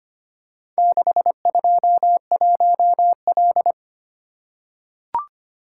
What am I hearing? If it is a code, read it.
Morse code, heard as a keyed tone, about 25 wpm, 700 Hz. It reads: 621L
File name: practice / 25wpm / Four Letter-Number Combinations - 25wpm